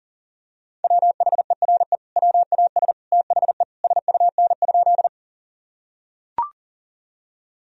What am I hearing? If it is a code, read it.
Morse code, heard as a keyed tone, about 40 wpm, 700 Hz. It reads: WHERE WAS THE SUN?
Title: Sentences from Top 1000 Words - Review Part 1 - 40wpm